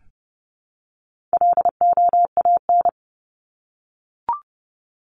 Morse code, heard as a keyed tone, about 30 wpm, 700 Hz. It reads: LOAN
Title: Top 1501-1750 Words 30wpm